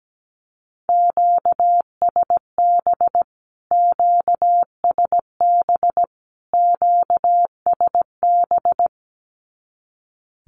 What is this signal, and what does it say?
Morse code, heard as a keyed tone, about 17 wpm, 700 Hz. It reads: QSB QSB QSB
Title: Q-Codes Repeated 3x 17wpm